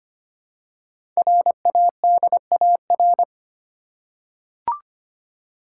Morse code, heard as a keyed tone, about 25 wpm, 700 Hz. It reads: RADAR
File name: 5 Letter Words - First 3 Letters Given - 25wpm